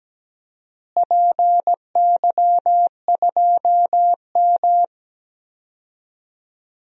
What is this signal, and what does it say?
Morse code, heard as a keyed tone, about 17 wpm, 700 Hz. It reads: PY2M
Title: DXCC-100 Call Signs - 4 Characters - Repeated 3x - 17wpm